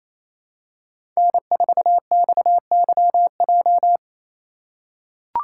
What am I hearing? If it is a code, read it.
Morse code, heard as a keyed tone, about 27 wpm, 700 Hz. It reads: N4XYJ
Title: USA Call Signs - 5 Characters - 4x Character Spacing - 28wpm